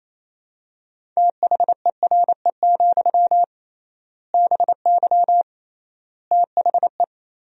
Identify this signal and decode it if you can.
Morse code, heard as a keyed tone, about 28 wpm, 700 Hz. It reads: THERE, BY THE
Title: 3x Word-Spacing - Sentences from Top 500 Words - 28wpm